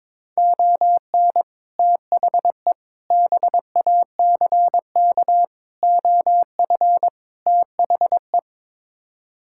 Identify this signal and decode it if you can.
Morse code, heard as a keyed tone, about 22 wpm, 700 Hz. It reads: ON THE BACK OF THE
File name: practice / 22wpm / Build A Phrase - 5-Word Phrases - 22wpm